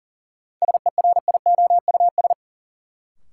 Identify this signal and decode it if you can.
Morse code, heard as a keyed tone, about 40 wpm, 700 Hz. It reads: SERIOUS
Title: Sentences from Top 1100 Words - New Word Drill - 40wpm